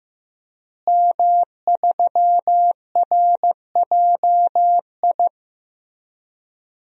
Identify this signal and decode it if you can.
Morse code, heard as a keyed tone, about 15 wpm, 700 Hz. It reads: M3RJI
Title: DXCC-100 Call Signs - 5 Characters - Repeated 2x - 15wpm